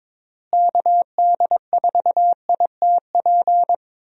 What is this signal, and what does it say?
Morse code, heard as a keyed tone, about 22 wpm, 700 Hz. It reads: KD4ITP
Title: USA Call Signs - 6 Characters - Repeated 2x - 22wpm